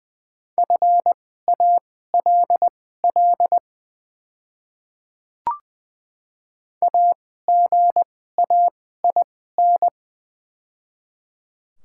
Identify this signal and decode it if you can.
Morse code, heard as a keyed tone, about 20 wpm, spaced as for 14 wpm, 700 Hz. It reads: FALL AGAIN
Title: 2x Character Spacing - Top 500 Words - 20wpm